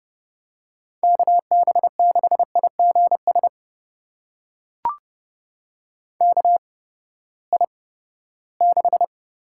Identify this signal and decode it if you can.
Morse code, heard as a keyed tone, about 30 wpm, spaced as for 17 wpm, 700 Hz. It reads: KB6IGS KI6
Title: USA Call Signs - 6 Characters - 8x Character Spacing - 30wpm